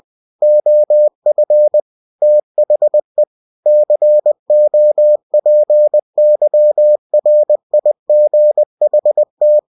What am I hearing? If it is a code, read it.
Morse code, heard as a keyed tone, about 20 wpm, 600 Hz. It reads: OF THE COPYRIGHT
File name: winnie-20wpm